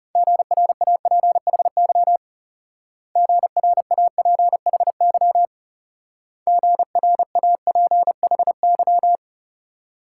Morse code, heard as a keyed tone, about 35 wpm, 700 Hz. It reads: GRAPHY GRAPHY GRAPHY
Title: Speed-Racing Multi-letter Phonemes - 30wpm